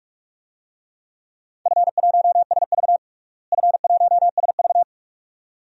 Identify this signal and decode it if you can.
Morse code, heard as a keyed tone, about 43 wpm, 700 Hz. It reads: F1SV F1SV
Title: DXCC-100 Call Signs - 4 Characters - Repeated 2x - 45wpm